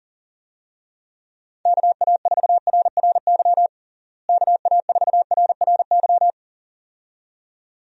Morse code, heard as a keyed tone, about 40 wpm, 700 Hz. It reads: KA4RRY KA4RRY
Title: USA Call Signs - 6 Characters - Repeated 2x - 40wpm